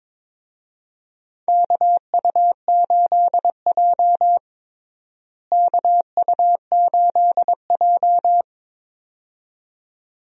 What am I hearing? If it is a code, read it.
Morse code, heard as a keyed tone, about 22 wpm, 700 Hz. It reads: KU8J KU8J
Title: USA Call Signs - 4 Characters - Repeated 2x - 22wpm